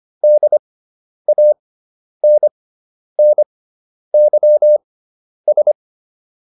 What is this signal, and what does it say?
Morse code, heard as a keyed tone, about 25 wpm, spaced as for 10 wpm, 600 Hz. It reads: DANNYS